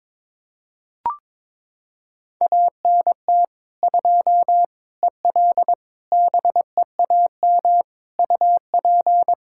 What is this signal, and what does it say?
Morse code, heard as a keyed tone, about 22 wpm, 700 Hz. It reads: ANT 2 EL BEAM UP
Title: Practice QSO - Exchange 2 - Info - 22wpm